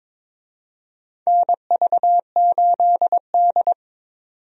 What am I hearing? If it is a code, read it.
Morse code, heard as a keyed tone, about 22 wpm, 700 Hz. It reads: NV8D
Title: USA Call Signs - 4 Characters - 22wpm